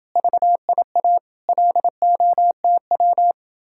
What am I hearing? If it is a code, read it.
Morse code, heard as a keyed tone, about 30 wpm, 700 Hz. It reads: VIA LOTW